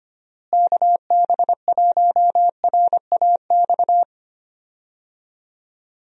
Morse code, heard as a keyed tone, about 25 wpm, 700 Hz. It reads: KB1RAX
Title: USA Call Signs - 6 Characters - Repeated 2x - 25wpm